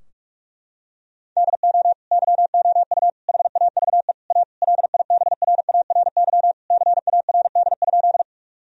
Morse code, heard as a keyed tone, about 46 wpm, 700 Hz. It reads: DO YOU HAVE A LIBRARY CARD?
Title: Sentences from Top 1000 Words - Review Part 4 - 45wpm